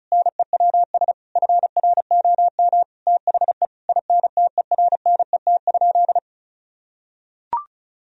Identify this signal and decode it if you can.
Morse code, heard as a keyed tone, about 36 wpm, 700 Hz. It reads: NEWS FROM THE INTERNET?
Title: Sentences from Top 1000 Words - Review Part 4 - 35wpm